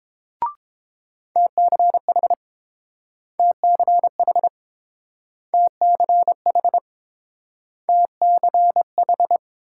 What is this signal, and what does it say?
Morse code, heard as a keyed tone, about 30 wpm, 700 Hz. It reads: TCH TCH TCH TCH